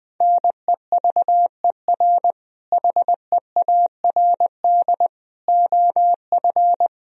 Morse code, heard as a keyed tone, about 20 wpm, 700 Hz. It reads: NEVER HEARD OF
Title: Sentences from Top 400 Words and Abbreviations 20wpm